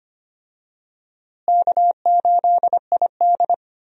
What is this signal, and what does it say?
Morse code, heard as a keyed tone, about 25 wpm, 700 Hz. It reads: K8ID
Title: USA Call Signs - 4 Characters - Repeated 2x - 25wpm